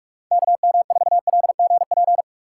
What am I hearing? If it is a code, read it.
Morse code, heard as a keyed tone, about 46 wpm, 700 Hz. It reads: K M 4 L G P
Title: USA Call Signs - 6 Characters - 6x Character Spacing - 45wpm